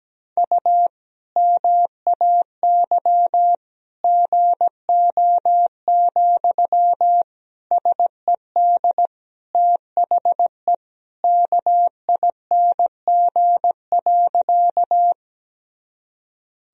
Morse code, heard as a keyed tone, about 17 wpm, 700 Hz. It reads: U MAY GO, SED THE KING.
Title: Sentences from Top 400 Words and Abbreviations 17wpm